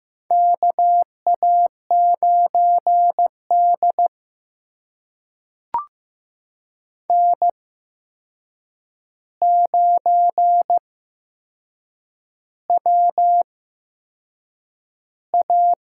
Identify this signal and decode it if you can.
Morse code, heard as a keyed tone, about 15 wpm, spaced as for 6 wpm, 700 Hz. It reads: KA9D N9WA
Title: USA Call Signs - 4 Characters - 8x Character Spacing - 15wpm